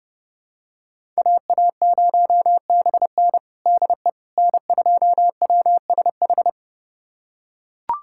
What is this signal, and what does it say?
Morse code, heard as a keyed tone, about 30 wpm, 700 Hz. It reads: AA0BN DE N2WSH
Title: USA Callsign DE Callsign - 5 Characters - 30wpm